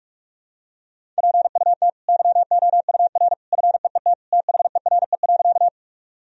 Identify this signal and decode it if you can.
Morse code, heard as a keyed tone, about 46 wpm, 700 Hz. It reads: PUT YOUR FEET THERE.
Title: Sentences from Top 1000 Words - Review Part 1 - 45wpm